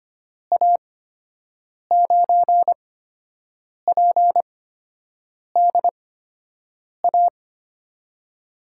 Morse code, heard as a keyed tone, about 25 wpm, spaced as for 7 wpm, 700 Hz. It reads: A9PDA
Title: USA Call Signs - 6 Characters - 8x Character Spacing - 25wpm